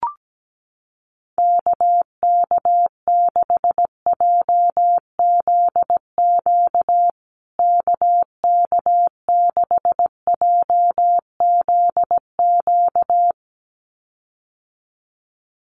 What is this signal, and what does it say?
Morse code, heard as a keyed tone, about 17 wpm, 700 Hz. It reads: KK6JZQ KK6JZQ
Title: USA Call Signs Repeated 2x 17wpm